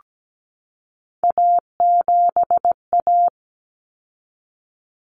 Morse code, heard as a keyed tone, about 17 wpm, 700 Hz. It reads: A7A
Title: Common Cut Numbers - 3-Digit 17wpm